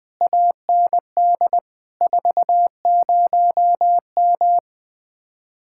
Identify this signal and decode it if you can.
Morse code, heard as a keyed tone, about 20 wpm, 700 Hz. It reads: AND 40M
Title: Speed-Racing Common QSO Phrases - 17wpm